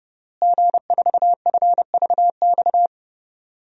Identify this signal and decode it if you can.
Morse code, heard as a keyed tone, about 30 wpm, 700 Hz. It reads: G4FVX
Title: DXCC-100 Call Signs - 5 Characters - Repeated 2x - 30wpm